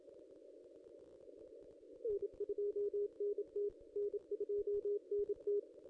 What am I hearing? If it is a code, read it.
Morse code, heard as a keyed tone, about 27 wpm, 410 Hz. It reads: N2NT N2NT
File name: contests / ss-cw-k5tr-op-n5rz -1600